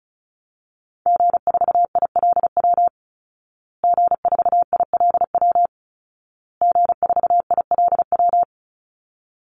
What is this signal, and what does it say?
Morse code, heard as a keyed tone, about 34 wpm, 700 Hz. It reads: G4ILW G4ILW G4ILW
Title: Call Signs Repeated 2x - 35wpm